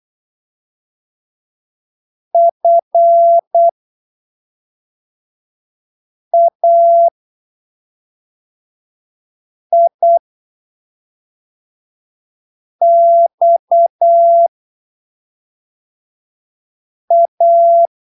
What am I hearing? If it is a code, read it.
Morse code, heard as a keyed tone, about 8 wpm, spaced as for 3 wpm, 680 Hz. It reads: FAIXA
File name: arquivos / texto / texto 12 08ppm